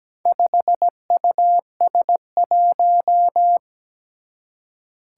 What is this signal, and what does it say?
Morse code, heard as a keyed tone, about 17 wpm, 700 Hz. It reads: 5US1
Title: Four Challenging Letter-Number Combinations 17wpm